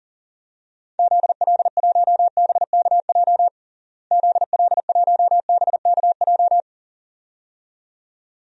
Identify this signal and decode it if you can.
Morse code, heard as a keyed tone, about 40 wpm, 700 Hz. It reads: ZL1BKJ ZL1BKJ
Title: DXCC-100 Call Signs - 6 Characters - Repeated 2x - 40wpm